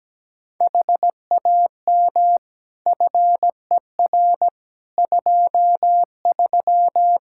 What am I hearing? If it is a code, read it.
Morse code, heard as a keyed tone, about 17 wpm, 700 Hz. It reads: HAM FER 23
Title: K1USN SST QSO Practice 17wpm